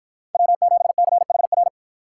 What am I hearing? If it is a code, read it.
Morse code, heard as a keyed tone, about 53 wpm, 700 Hz. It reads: W7CHR